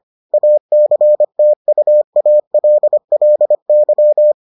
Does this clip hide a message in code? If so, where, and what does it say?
Yes: Morse code from 0.3 to 4.3 s, heard as a keyed tone, about 25 wpm, 600 Hz. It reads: ACTUALLY